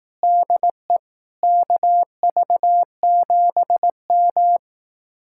Morse code, heard as a keyed tone, about 18 wpm, 700 Hz. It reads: DE KV7M
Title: Speed-Racing Common QSO Phrases Encore - 15wpm